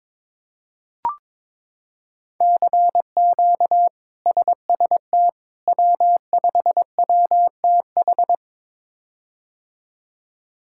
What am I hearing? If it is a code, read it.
Morse code, heard as a keyed tone, about 22 wpm, 700 Hz. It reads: CQ SST W5WTH
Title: K1USN SST QSO Practice - 22wpm